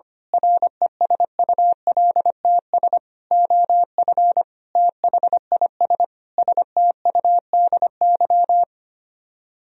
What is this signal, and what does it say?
Morse code, heard as a keyed tone, about 24 wpm, 700 Hz. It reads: RESULTS OF THIS STUDY
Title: Speed-Racing - Sets of 5 Words 25wpm